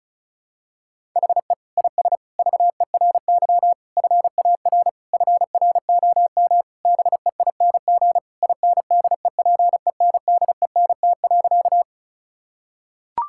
Morse code, heard as a keyed tone, about 36 wpm, 700 Hz. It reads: HE IS VERY FAR FROM BEING INDEPENDENT.
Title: Sentences from Top 1000 Words - Review Part 6 - 35wpm